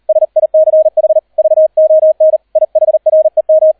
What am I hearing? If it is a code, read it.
Morse code, heard as a keyed tone, about 38 wpm, 630 Hz. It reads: SICH VON IHREM